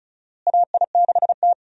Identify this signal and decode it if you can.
Morse code, heard as a keyed tone, about 36 wpm, 700 Hz. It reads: AI6T
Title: USA Call Signs - 4 Characters - Repeated 3x - 35wpm